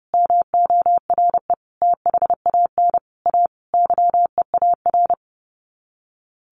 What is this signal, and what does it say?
Morse code, heard as a keyed tone, about 30 wpm, 700 Hz. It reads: MORE THAN A YEAR
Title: Sets of 5 Words 30wpm